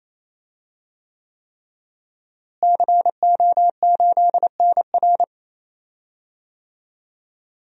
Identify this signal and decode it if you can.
Morse code, heard as a keyed tone, about 28 wpm, 700 Hz. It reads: CO8NR
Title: DXCC-100 Call Signs - 5 Characters - Repeated 3x - 28wpm